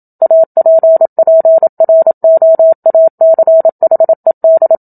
Morse code, heard as a keyed tone, about 27 wpm, 650 Hz. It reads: APPROACHED
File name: Book Two / 01-Chapter 1 - 27 WPM